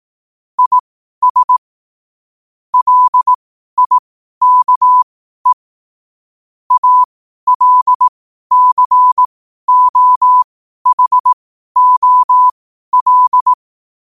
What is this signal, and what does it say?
Morse code, heard as a keyed tone, about 18 wpm, spaced as for 12 wpm, 1000 Hz. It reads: IS LIKE ALCOHOL
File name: qotdfiles / 12wpm